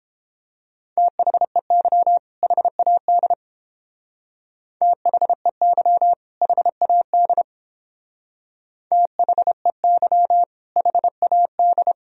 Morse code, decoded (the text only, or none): THEY HAD THEY HAD THEY HAD